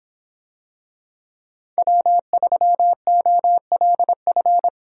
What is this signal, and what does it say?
Morse code, heard as a keyed tone, about 25 wpm, 700 Hz. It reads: W3OLF